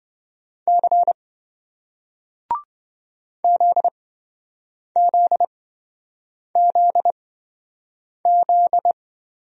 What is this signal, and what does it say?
Morse code, heard as a keyed tone, about 27 wpm, spaced as for 7 wpm, 700 Hz. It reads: C ZZZZ